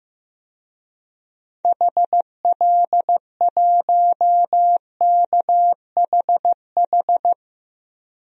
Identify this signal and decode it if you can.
Morse code, heard as a keyed tone, about 15 wpm, 700 Hz. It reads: HL1KHH
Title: DXCC-100 Call Signs - 6 Characters - 15wpm